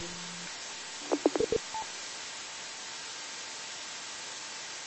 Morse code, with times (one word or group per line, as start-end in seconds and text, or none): none